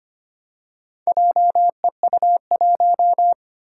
Morse code, heard as a keyed tone, about 25 wpm, 700 Hz. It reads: JEU1
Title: Four Challenging Letter-Number Combinations 25wpm